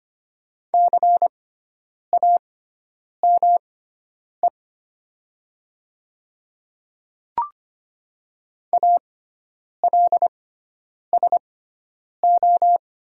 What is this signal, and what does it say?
Morse code, heard as a keyed tone, about 25 wpm, spaced as for 9 wpm, 690 Hz. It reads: CAME ALSO
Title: 6x Character Spacing - Top 200 Words - 25wpm